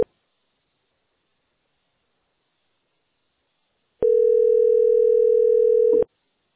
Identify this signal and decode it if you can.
Ringback tone (North America), 440 + 480 Hz.